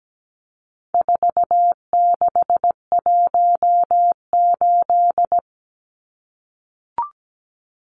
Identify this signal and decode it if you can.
Morse code, heard as a keyed tone, about 17 wpm, 700 Hz. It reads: 4618
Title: Numbers - 4-Digit 17wpm